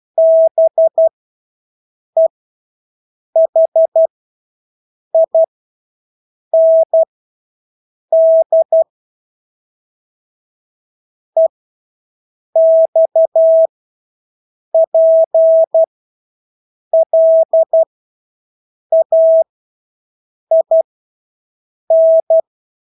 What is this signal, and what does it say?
Morse code, heard as a keyed tone, about 12 wpm, spaced as for 6 wpm, 650 Hz. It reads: BEHIND EXPLAIN
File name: Engelske ord 12-6 wpm